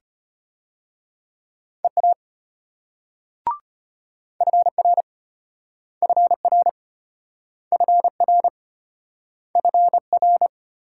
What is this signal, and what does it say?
Morse code, heard as a keyed tone, about 30 wpm, 700 Hz. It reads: EA FR FR FR FR
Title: Speed-Racing Multi-letter Phonemes - 25wpm